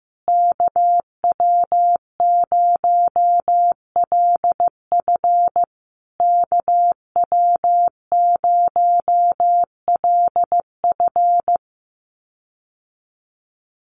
Morse code, heard as a keyed tone, about 15 wpm, 700 Hz. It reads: KW0LF KW0LF
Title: USA Call Signs Repeated 3x 15wpm